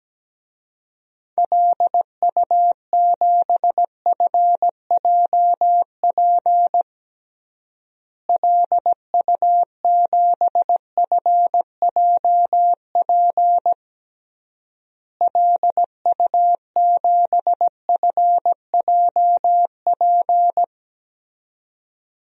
Morse code, heard as a keyed tone, about 17 wpm, 700 Hz. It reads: LU7FJP LU7FJP LU7FJP
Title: DXCC-100 Call Signs - 6 Characters - Repeated 3x - 17wpm